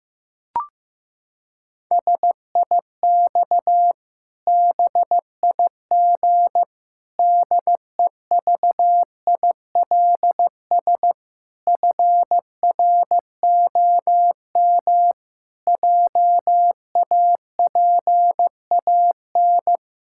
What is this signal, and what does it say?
Morse code, heard as a keyed tone, about 15 wpm, 700 Hz. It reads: SIX BIG DEVILS FROM JAPAN
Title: Pangram Challenge v2 - 15wpm